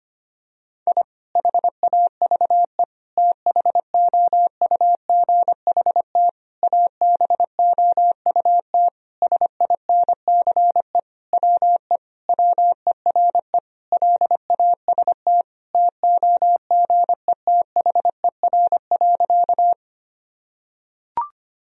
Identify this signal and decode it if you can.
Morse code, heard as a keyed tone, about 25 wpm, 700 Hz. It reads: I HAVE THOUGHT ABOUT SINCE WE WERE LAST TOGETHER.
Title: Sentences from Top 1000 Words - Review Part 1 - 25wpm